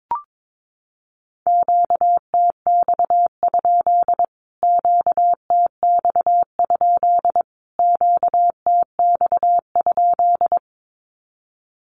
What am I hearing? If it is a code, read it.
Morse code, heard as a keyed tone, about 22 wpm, 700 Hz. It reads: QTX? QTX? QTX?
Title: Q-Codes Repeated 3x 22wpm